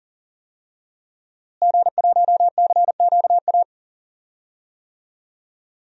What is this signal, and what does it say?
Morse code, heard as a keyed tone, about 40 wpm, 700 Hz. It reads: G1CQA
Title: DXCC-100 Call Signs - 5 Characters - Repeated 3x - 40wpm